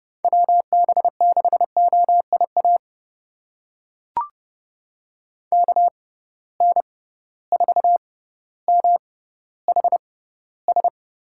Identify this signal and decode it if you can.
Morse code, heard as a keyed tone, about 30 wpm, spaced as for 15 wpm, 700 Hz. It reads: WB6OIA KN4MHS
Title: USA Call Signs - 6 Characters - 6x Character Spacing - 30wpm